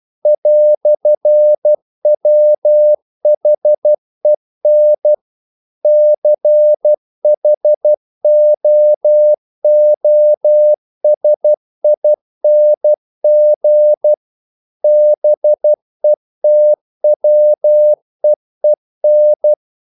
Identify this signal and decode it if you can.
Morse code, heard as a keyed tone, about 12 wpm, 600 Hz. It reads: "WHEN CHOOSING BETWEEN
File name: CW / Practice / EnglishQuotations0312